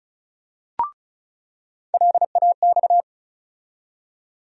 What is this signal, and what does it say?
Morse code, heard as a keyed tone, about 34 wpm, 700 Hz. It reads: LAX
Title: USA Call Sign Postfixes - 3 Letters 35wpm